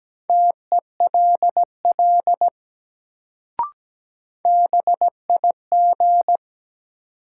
Morse code, heard as a keyed tone, about 17 wpm, 700 Hz. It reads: TELL BIG
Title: Sentences from Top 200 Words - New Word Drill - 17wpm